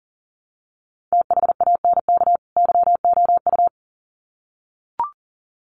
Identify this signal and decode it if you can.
Morse code, heard as a keyed tone, about 40 wpm, 700 Hz. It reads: THANK YOU
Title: Sets of 2 Words 40wpm